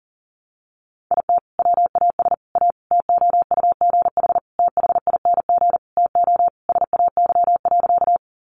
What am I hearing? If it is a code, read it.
Morse code, heard as a keyed tone, about 40 wpm, 700 Hz. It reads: IT WAS A TOUGH THING TO SAY.
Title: Sentences from Top 1200 Words 40wpm